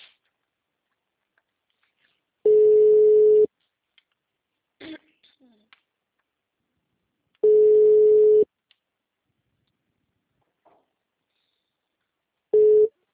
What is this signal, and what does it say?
Ringback tone (Europe, 425 Hz), with speech.